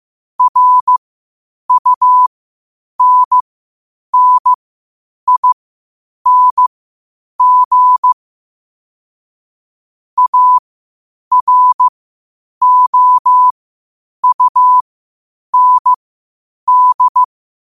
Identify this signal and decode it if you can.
Morse code, heard as a keyed tone, about 15 wpm, spaced as for 8 wpm, 1000 Hz. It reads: RUNNING AROUND
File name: qotdfiles / 8wpm